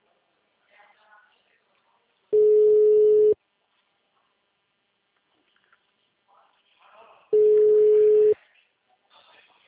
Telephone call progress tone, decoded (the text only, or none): ringback tone